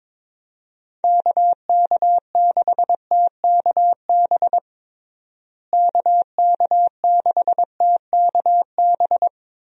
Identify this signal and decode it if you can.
Morse code, heard as a keyed tone, about 22 wpm, 700 Hz. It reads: KK6TKB KK6TKB
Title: USA Call Signs - 6 Characters - Repeated 3x - 22wpm